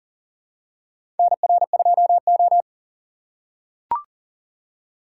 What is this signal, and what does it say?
Morse code, heard as a keyed tone, about 40 wpm, 700 Hz. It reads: NR2O
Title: USA Call Signs - 4 Characters - 3x Character Spacing - 40wpm